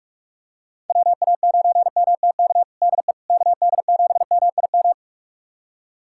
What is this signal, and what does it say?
Morse code, heard as a keyed tone, about 42 wpm, 700 Hz. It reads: WA9MTK DE KD7MIM
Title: USA Callsign DE Callsign - 6 Characters - 45wpm